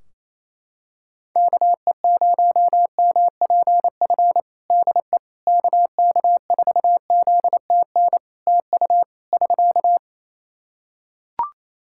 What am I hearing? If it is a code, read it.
Morse code, heard as a keyed tone, about 28 wpm, 700 Hz. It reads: KE0MPF DE KK4ZTN TU <SK>
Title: Practice QSO - Exchange 4 - Ending - 28wpm